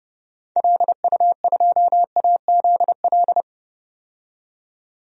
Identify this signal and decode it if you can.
Morse code, heard as a keyed tone, about 31 wpm, 700 Hz. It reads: LU2AZL
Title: DXCC-100 Call Signs - 6 Characters - 30wpm